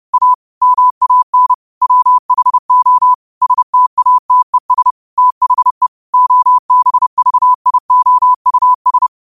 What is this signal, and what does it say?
Morse code, heard as a keyed tone, about 30 wpm, 1000 Hz. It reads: A MAN WHO STATES THE OBVIOUS
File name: qotdfiles / 30wpm